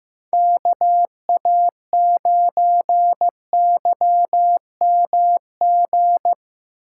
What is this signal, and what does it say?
Morse code, heard as a keyed tone, about 15 wpm, 700 Hz. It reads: KA9YMG
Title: K1USN SST QSO Practice - 15wpm